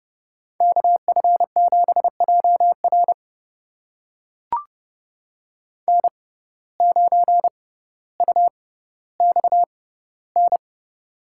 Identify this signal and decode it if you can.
Morse code, heard as a keyed tone, about 30 wpm, spaced as for 15 wpm, 700 Hz. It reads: KF7JR N9UXN
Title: USA Call Signs - 5 Characters - 6x Character Spacing - 30wpm